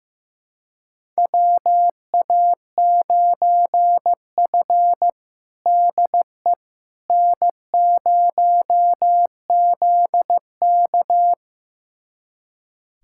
Morse code, heard as a keyed tone, about 15 wpm, 700 Hz. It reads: WA9F DE N0ZK
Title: USA Callsign DE Callsign - 4 Characters - 15wpm